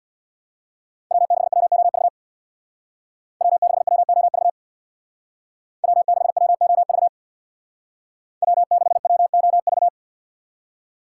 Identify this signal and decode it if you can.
Morse code, heard as a keyed tone, about 63 wpm, 700 Hz. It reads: W6WOV W6WOV W6WOV W6WOV